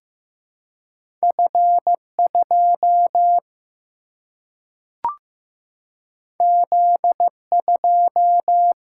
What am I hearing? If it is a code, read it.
Morse code, heard as a keyed tone, about 15 wpm, 700 Hz. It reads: F 2 Z 2
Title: Single-Letter Single-Number 15wpm